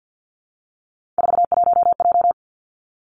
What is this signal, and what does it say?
Morse code, heard as a keyed tone, about 50 wpm, 700 Hz. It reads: 41J
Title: Three Challenging Letter-Number Combinations 50wpm